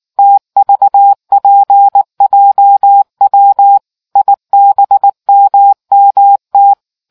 Morse code, heard as a keyed tone, about 19 wpm, 800 Hz. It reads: TVPJW IBMMT